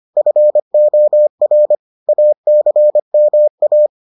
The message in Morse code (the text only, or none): FOR ACMA